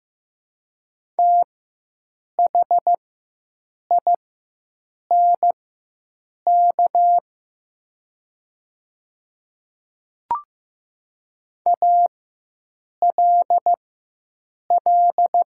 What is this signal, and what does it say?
Morse code, heard as a keyed tone, about 15 wpm, spaced as for 7 wpm, 700 Hz. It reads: THINK ALL
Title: 4x Character Spacing - Top 300 Words - 15wpm